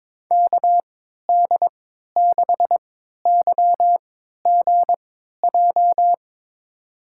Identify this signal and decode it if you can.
Morse code, heard as a keyed tone, about 22 wpm, spaced as for 13 wpm, 700 Hz. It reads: KD6YGJ
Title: USA Call Signs - 3x Character Spacing - 22wpm